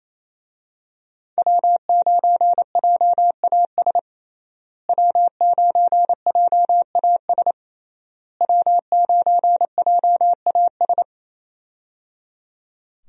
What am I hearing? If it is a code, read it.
Morse code, heard as a keyed tone, about 27 wpm, 700 Hz. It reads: W9JAS W9JAS W9JAS
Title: USA Call Signs - 5 Characters - Repeated 3x - 28wpm